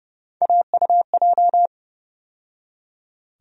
Morse code, heard as a keyed tone, about 31 wpm, 700 Hz. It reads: AUJ
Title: USA Call Sign Postfixes - 3 Letters 30wpm